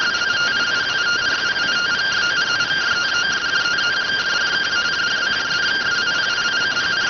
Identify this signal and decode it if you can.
Two-tone teleprinter signal, about 170 Hz shift.